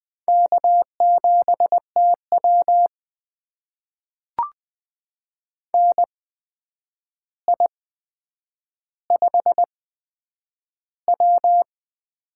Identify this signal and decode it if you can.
Morse code, heard as a keyed tone, about 20 wpm, spaced as for 9 wpm, 700 Hz. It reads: K7TW NI5W
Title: USA Call Signs - 4 Characters - 8x Character Spacing - 20wpm